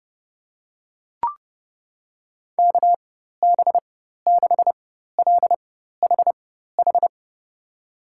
Morse code, heard as a keyed tone, about 29 wpm, spaced as for 14 wpm, 700 Hz. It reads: KB6LHH
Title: USA Call Signs - 6 Characters - 4x Character Spacing - 30wpm